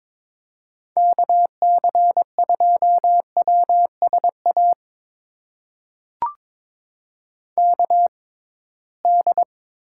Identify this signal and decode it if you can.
Morse code, heard as a keyed tone, about 22 wpm, spaced as for 17 wpm, 700 Hz. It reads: KC2WSA KD
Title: USA Call Signs - 6 Characters - 6x Character Spacing - 22wpm